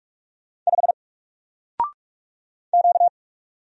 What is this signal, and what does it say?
Morse code, heard as a keyed tone, about 44 wpm, 700 Hz. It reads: 5 Q
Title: Single Letter-Number 45wpm